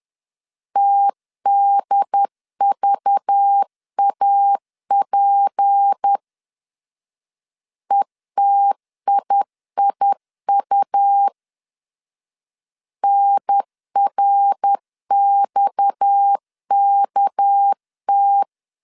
Morse code, heard as a keyed tone, about 10 wpm, 800 Hz. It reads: TDVAP ETIIU NRXKT
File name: CW / DET / 1 6 1500C S B 10wpm 4137